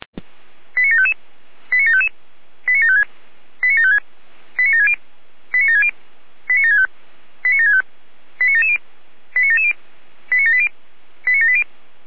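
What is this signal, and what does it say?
Five-tone selective call (ZVEI), heard as quick runs of five tones, tones of about 70 ms. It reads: 89755 89755 89756 89756 89760 89760 89765 89765 89770 89770 89780 89780
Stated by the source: ZVEI five-tone call 89725 received by SDR.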